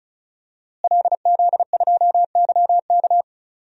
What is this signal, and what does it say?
Morse code, heard as a keyed tone, about 36 wpm, 700 Hz. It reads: LZ2YK
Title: DXCC-100 Call Signs - 5 Characters - Repeated 2x - 35wpm